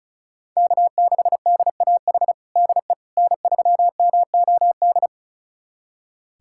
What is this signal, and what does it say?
Morse code, heard as a keyed tone, about 36 wpm, 700 Hz. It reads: K6DAH DE N3MOD
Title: USA Callsign DE Callsign - 5 Characters - 35wpm